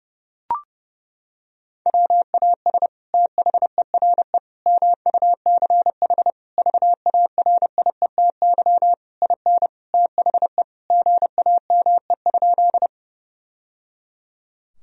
Morse code, heard as a keyed tone, about 32 wpm, 700 Hz. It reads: WAS THERE MUCH VARIETY IN THE GAME?
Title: Sentences from Top 1000 Words - Review Part 4 - 30wpm